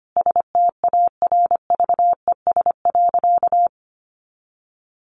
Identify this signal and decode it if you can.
Morse code, heard as a keyed tone, about 25 wpm, 700 Hz. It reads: STARVES.
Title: A Sound Of Thunder 25wpm